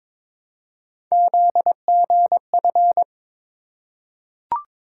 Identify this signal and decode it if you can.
Morse code, heard as a keyed tone, about 23 wpm, 700 Hz. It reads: ZGF
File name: practice / 22wpm / Three Letter-Number Combinations - 22wpm